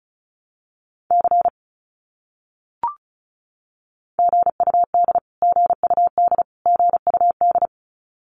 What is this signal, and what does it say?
Morse code, heard as a keyed tone, about 34 wpm, 700 Hz. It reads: C GUD GUD GUD
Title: Abbreviations Repeated 3x - 35wpm